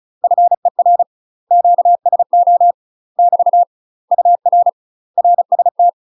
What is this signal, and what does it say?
Morse code, heard as a keyed tone, about 36 wpm, 700 Hz. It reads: FER QSO <BT> UR RST
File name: UK MASTER QSOs TRAINING / 35WPM / UK MASTER QSOs 0033